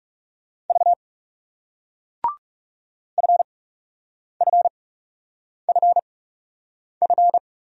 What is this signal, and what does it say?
Morse code, heard as a keyed tone, about 39 wpm, spaced as for 8 wpm, 700 Hz. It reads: V FFFF